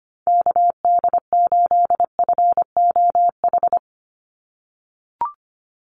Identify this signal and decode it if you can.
Morse code, heard as a keyed tone, about 25 wpm, 700 Hz. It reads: KD8FOH
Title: USA Call Signs 25wpm